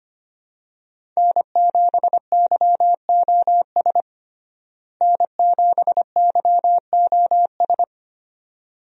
Morse code, heard as a keyed tone, about 25 wpm, 700 Hz. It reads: N7YOS N7YOS
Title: USA Call Signs - 5 Characters - Repeated 2x - 25wpm